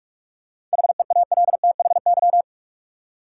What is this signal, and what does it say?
Morse code, heard as a keyed tone, about 46 wpm, 700 Hz. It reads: HEALTHY
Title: Seven Letter Words - 45wpm